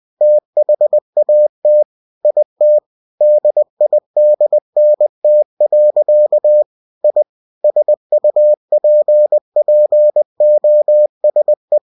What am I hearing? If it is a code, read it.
Morse code, heard as a keyed tone, about 20 wpm, 600 Hz. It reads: THAT IT DIDNT. I SUPPOSE